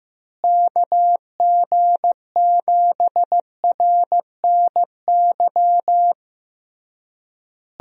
Morse code, heard as a keyed tone, about 16 wpm, 700 Hz. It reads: KG7RNY
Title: USA Call Signs - 6 Characters - Repeated 3x - 15wpm